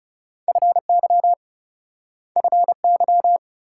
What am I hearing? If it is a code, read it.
Morse code, heard as a keyed tone, about 41 wpm, 700 Hz. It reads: FY FY